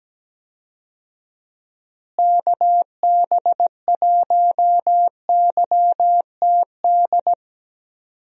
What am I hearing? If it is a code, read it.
Morse code, heard as a keyed tone, about 17 wpm, 700 Hz. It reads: KB1YTD
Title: USA Call Signs - 6 Characters - Repeated 2x - 17wpm